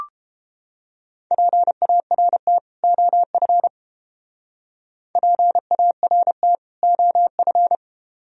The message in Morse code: PART OF PART OF

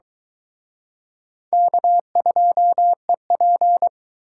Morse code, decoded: K2EP

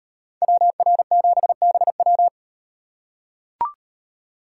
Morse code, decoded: WR7BW

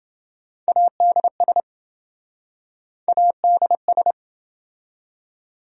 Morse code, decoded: ADS ADS

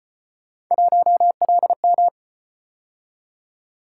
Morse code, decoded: 1LM

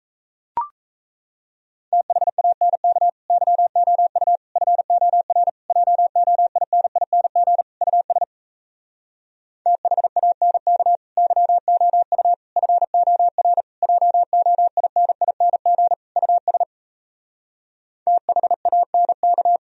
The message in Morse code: THANK YOU FOR JOINING US THANK YOU FOR JOINING US THANK

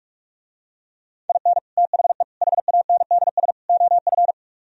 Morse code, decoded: IN THE HANDS OF

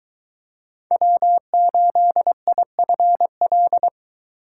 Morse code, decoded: W8IFL

